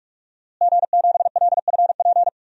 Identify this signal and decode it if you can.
Morse code, heard as a keyed tone, about 43 wpm, 700 Hz. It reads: G7LFP
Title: DXCC-100 Call Signs - 5 Characters - Repeated 3x - 45wpm